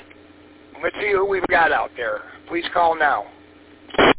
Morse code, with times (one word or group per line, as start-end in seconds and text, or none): none